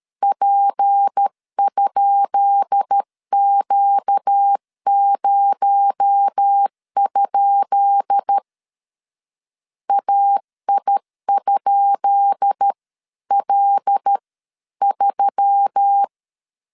Morse code, heard as a keyed tone, about 13 wpm, spaced as for 11 wpm, 800 Hz. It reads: P?Q0? AI?L3
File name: LEZIONI / 10 SIGNS / 10B INT 12wpm 2414